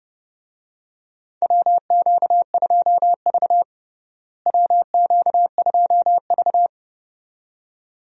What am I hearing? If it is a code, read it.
Morse code, heard as a keyed tone, about 32 wpm, 700 Hz. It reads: WQ2V WQ2V
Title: USA Call Signs - 4 Characters - Repeated 2x - 30wpm